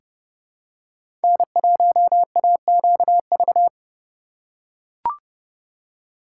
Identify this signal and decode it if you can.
Morse code, heard as a keyed tone, about 31 wpm, 700 Hz. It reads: N1AQV
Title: USA Call Signs - 4x Character Spacing - 30wpm